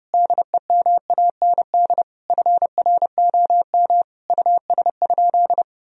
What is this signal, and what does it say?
Morse code, heard as a keyed tone, about 30 wpm, 700 Hz. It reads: DEMAND FROM US?
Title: Sentences from Top 1000 Words - Review Part 4 - 30wpm